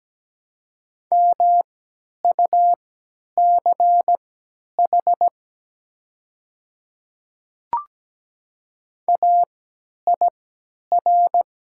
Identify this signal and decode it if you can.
Morse code, heard as a keyed tone, about 17 wpm, spaced as for 10 wpm, 700 Hz. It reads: MUCH AIR
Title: 3x Character Spacing - Top 200 Words - 17wpm